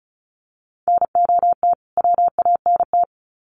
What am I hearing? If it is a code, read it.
Morse code, heard as a keyed tone, about 34 wpm, 700 Hz. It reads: NOT WANT
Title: Sets of 2 Words 35wpm